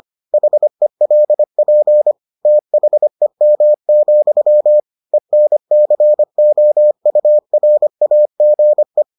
Morse code, heard as a keyed tone, about 25 wpm, 600 Hz. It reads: HELP THEM, ENCOURAGE